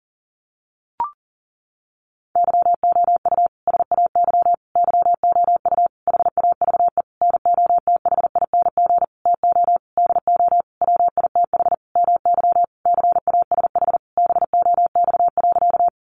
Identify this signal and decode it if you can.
Morse code, heard as a keyed tone, about 40 wpm, 700 Hz. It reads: YOU SAY YOU HAVE NOTHING TO DO WITH MY CASH BOX.
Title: Sentences from Top 700 Words 40wpm